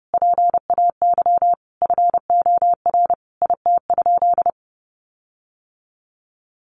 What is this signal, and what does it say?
Morse code, heard as a keyed tone, about 30 wpm, 700 Hz. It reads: PAY FOR IT?
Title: Sentences from Top 1300 Words 30wpm